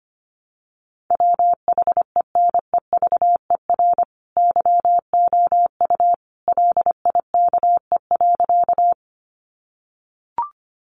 Morse code, heard as a keyed tone, about 25 wpm, 700 Hz. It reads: WHENEVER YOU LIKE.
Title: Sentences from Top 1200 Words 25wpm